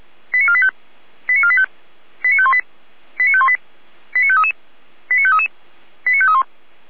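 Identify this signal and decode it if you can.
Five-tone selective call (ZVEI), heard as quick runs of five tones, tones of about 70 ms. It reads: 89485 89485 89518 89518 89533 89533 89531
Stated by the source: ZVEI five-tone selective call received with an SDR.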